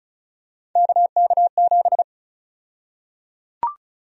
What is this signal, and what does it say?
Morse code, heard as a keyed tone, about 34 wpm, 700 Hz. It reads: KK7
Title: USA Call Sign Prefixes - 35wpm